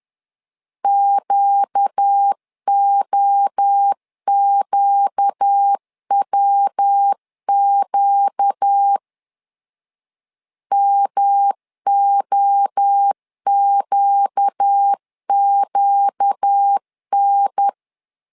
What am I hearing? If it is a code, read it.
Morse code, heard as a keyed tone, about 10 wpm, 800 Hz. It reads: QOQWQ MOQQN
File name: LEZIONI / 4 Q 10wpm 2210